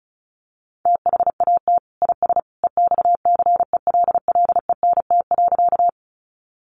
Morse code, heard as a keyed tone, about 34 wpm, 700 Hz. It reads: THAT IS EXCELLENT.
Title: Sentences from Top 1000 Words 35wpm